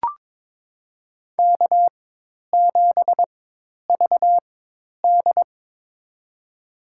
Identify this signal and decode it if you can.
Morse code, heard as a keyed tone, about 22 wpm, spaced as for 10 wpm, 700 Hz. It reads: K7VD
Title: USA Call Signs - 4 Characters - 4x Character Spacing - 22wpm